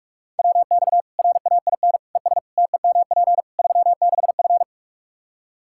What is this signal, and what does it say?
Morse code, heard as a keyed tone, about 46 wpm, 700 Hz. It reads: WX RAIN ES TEMP 36F
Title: Practice QSO - Exchange 2 - Info - 45wpm